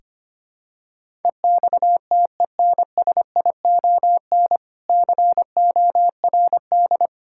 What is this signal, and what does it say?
Morse code, heard as a keyed tone, about 25 wpm, 700 Hz. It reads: EXTENSION CORD
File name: practice / 25wpm / Difficult Words with Hints - 25wpm